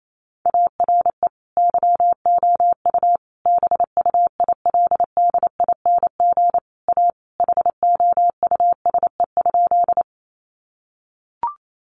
Morse code, heard as a keyed tone, about 28 wpm, 700 Hz. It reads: ARE YOU BUILDING A HOUSE?
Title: Sentences from Top 700 Words 28wpm